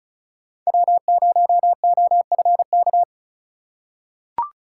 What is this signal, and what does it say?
Morse code, heard as a keyed tone, about 34 wpm, 700 Hz. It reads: W0OFK
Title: USA Call Signs - 5 Characters - 3x Character Spacing - 35wpm